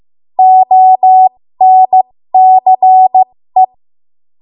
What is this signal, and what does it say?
Morse code, heard as a keyed tone, about 15 wpm, spaced as for 13 wpm, 750 Hz. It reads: ONCE
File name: Morse / 13WPM